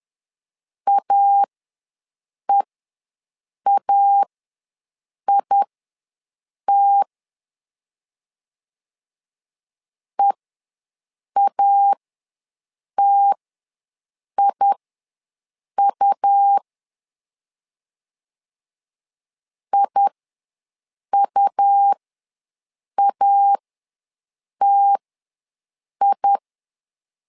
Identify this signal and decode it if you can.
Morse code, heard as a keyed tone, about 10 wpm, spaced as for 6 wpm, 800 Hz. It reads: AEAIT EATIU IUATI
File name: LEZIONI / 2 UHDN / 2 UH ws 10wpm 2029